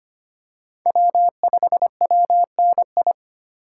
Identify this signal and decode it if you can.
Morse code, heard as a keyed tone, about 25 wpm, 700 Hz. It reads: W5WNI
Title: USA Call Signs - 5 Characters - 3x Character Spacing - 25wpm